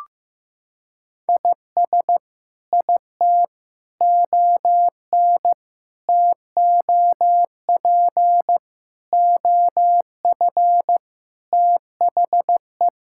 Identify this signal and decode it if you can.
Morse code, heard as a keyed tone, about 16 wpm, 700 Hz. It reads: IS IT ON TOP OF THE
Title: Sentences from Top 1000 Words - Review Part 6 - 15wpm